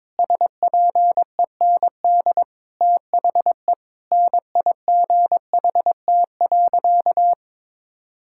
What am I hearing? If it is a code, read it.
Morse code, heard as a keyed tone, about 22 wpm, 700 Hz. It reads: SPEND THE NIGHT.
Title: Sentences from Top 1000 Words - Review Part 5 - 22wpm